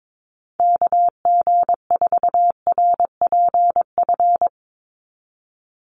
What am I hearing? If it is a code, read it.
Morse code, heard as a keyed tone, about 22 wpm, 700 Hz. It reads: KG4RPF
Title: USA Call Signs 22wpm